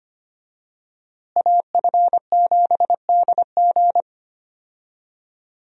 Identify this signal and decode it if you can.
Morse code, heard as a keyed tone, about 25 wpm, 700 Hz. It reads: AF7DG
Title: USA Call Signs - 5 Characters - Repeated 2x - 25wpm